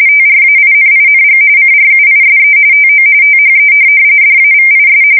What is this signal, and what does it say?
RTTY, about 45.45 baud. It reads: TRAINING CRUISE TO THE DAVIS
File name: fsk170 / USS-Missouri